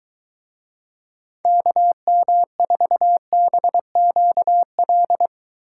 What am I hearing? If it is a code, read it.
Morse code, heard as a keyed tone, about 23 wpm, 700 Hz. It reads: KM4BQL